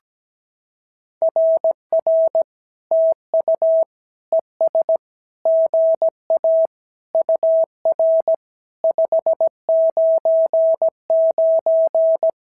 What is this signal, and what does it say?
Morse code, heard as a keyed tone, about 17 wpm, 650 Hz. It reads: RR TU ES GA UR 599
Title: SOTA QSO Practice 17wpm